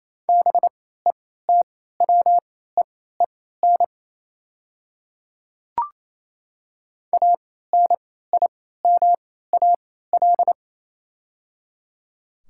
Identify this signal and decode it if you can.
Morse code, heard as a keyed tone, about 28 wpm, spaced as for 16 wpm, 700 Hz. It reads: BETWEEN ANIMAL
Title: 3x Character Spacing - Top 300 Words - 28wpm